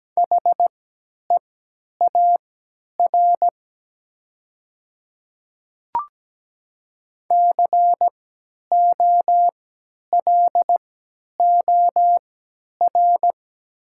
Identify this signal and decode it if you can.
Morse code, heard as a keyed tone, about 17 wpm, spaced as for 10 wpm, 700 Hz. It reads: HEAR COLOR
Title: 3x Character Spacing - Top 300 Words - 17wpm